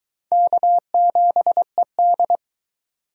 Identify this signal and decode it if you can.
Morse code, heard as a keyed tone, about 22 wpm, 700 Hz. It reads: K7ED